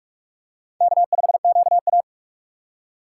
Morse code, heard as a keyed tone, about 44 wpm, 690 Hz. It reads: K5QA
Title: USA Call Signs - 4 Characters - 4x Character Spacing - 45wpm